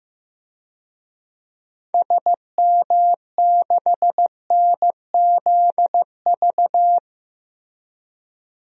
Morse code, heard as a keyed tone, about 15 wpm, 700 Hz. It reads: SM6NZV
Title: DXCC-100 Call Signs - 6 Characters - 15wpm